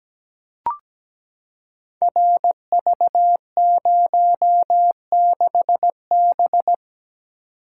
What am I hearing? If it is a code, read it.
Morse code, heard as a keyed tone, about 17 wpm, 700 Hz. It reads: RV06B
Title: Code Groups - 5 Letter:Numbers - 17wpm